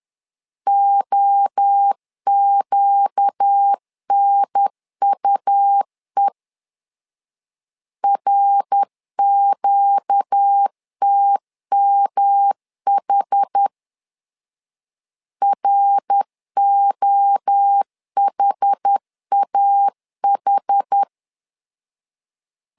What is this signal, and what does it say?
Morse code, heard as a keyed tone, about 10 wpm, 800 Hz. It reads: OQNUE RQTMH ROHAH
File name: LEZIONI / 4 Q 10wpm 2210